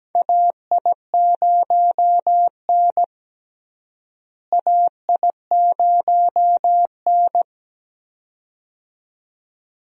Morse code, heard as a keyed tone, about 17 wpm, 700 Hz. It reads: AI0N AI0N
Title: USA Call Signs - 4 Characters - Repeated 2x - 17wpm